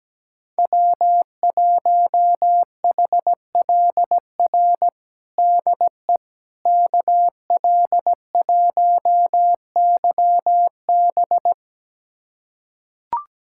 Morse code, heard as a keyed tone, about 17 wpm, 700 Hz. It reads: W1HLR DE KL1YB
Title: USA Callsign DE Callsign - 5 Characters - 17wpm